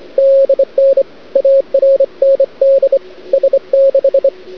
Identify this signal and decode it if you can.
Morse code, heard as a keyed tone, about 25 wpm, 540 Hz. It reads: DN ARND S6
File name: cd-odj / Suara / cwrag